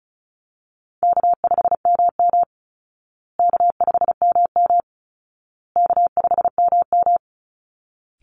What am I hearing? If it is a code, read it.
Morse code, heard as a keyed tone, about 34 wpm, 700 Hz. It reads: K5MM K5MM K5MM
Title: Call Signs Repeated 2x - 35wpm